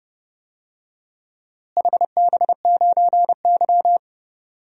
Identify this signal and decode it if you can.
Morse code, heard as a keyed tone, about 30 wpm, 700 Hz. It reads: HB9Y
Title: DXCC-100 Call Signs - 4 Characters - Repeated 3x - 30wpm